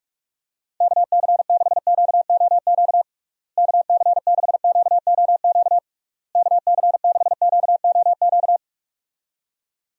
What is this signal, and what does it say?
Morse code, heard as a keyed tone, about 46 wpm, 700 Hz. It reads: KC6QOQ KC6QOQ KC6QOQ
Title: USA Call Signs - 6 Characters - Repeated 3x - 45wpm